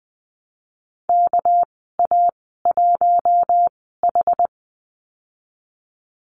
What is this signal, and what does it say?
Morse code, heard as a keyed tone, about 20 wpm, spaced as for 15 wpm, 700 Hz. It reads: KA1H
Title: 200 Lesson 32g - Letter k - Call Signs 20wpm